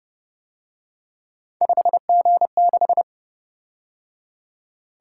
Morse code, heard as a keyed tone, about 30 wpm, 700 Hz. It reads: 5G6